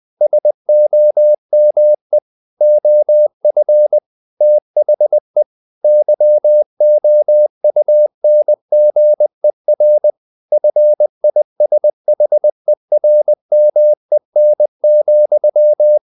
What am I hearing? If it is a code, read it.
Morse code, heard as a keyed tone, about 20 wpm, 600 Hz. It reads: SOME OF THE YOUNGER FISHERMEN,
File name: the old man and the sea / the old man and the sea part 1 20wpm